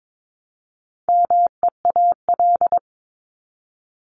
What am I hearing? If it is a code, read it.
Morse code, heard as a keyed tone, about 22 wpm, 700 Hz. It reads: MEAL
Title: Top 1751-2000 Words 22wpm